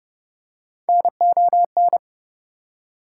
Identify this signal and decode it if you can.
Morse code, heard as a keyed tone, about 29 wpm, 700 Hz. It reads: NON
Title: Prefixes and Suffixes 30wpm